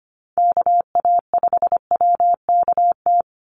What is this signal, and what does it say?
Morse code, heard as a keyed tone, about 25 wpm, 700 Hz. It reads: KA5WKT
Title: USA Call Signs 25wpm